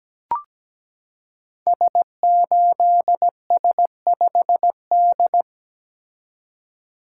Morse code, heard as a keyed tone, about 17 wpm, 700 Hz. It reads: S8S5D
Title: Code Groups - 5 Letter:Numbers - 17wpm